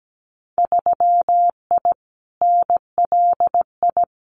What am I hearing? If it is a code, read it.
Morse code, heard as a keyed tone, about 17 wpm, 700 Hz. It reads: 3I NLI
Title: Winter Field Day Exchanges Encore 17wpm